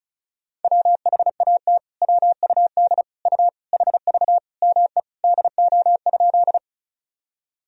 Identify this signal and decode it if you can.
Morse code, heard as a keyed tone, about 36 wpm, 700 Hz. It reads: WHAT WUD U HV ME DO?
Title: Sentences from Top 200 Words and Abbreviations 35wpm